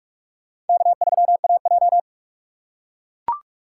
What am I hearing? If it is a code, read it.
Morse code, heard as a keyed tone, about 42 wpm, 700 Hz. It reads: K3AJ